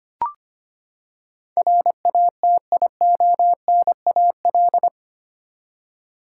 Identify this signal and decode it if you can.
Morse code, heard as a keyed tone, about 25 wpm, 700 Hz. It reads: RATIONAL
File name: Eight Letter Words - 25wpm